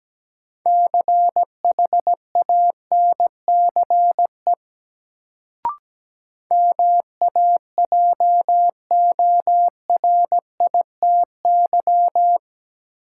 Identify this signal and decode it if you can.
Morse code, heard as a keyed tone, about 17 wpm, 700 Hz. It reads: CHANCE MAJORITY
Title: Sets of 3 Words - New Word Drill - 17wpm